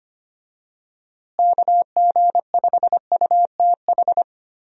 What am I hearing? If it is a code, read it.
Morse code, heard as a keyed tone, about 25 wpm, 700 Hz. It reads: KG5UTH
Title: USA Call Signs - 6 Characters - Repeated 3x - 25wpm